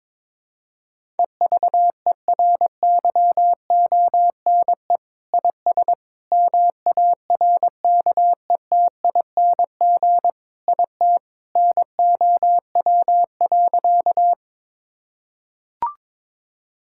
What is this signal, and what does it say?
Morse code, heard as a keyed tone, about 22 wpm, 700 Hz. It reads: EVERYONE IS MARKETING IT NOW.
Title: Sentences from Top 1000 Words - Review Part 5 - 22wpm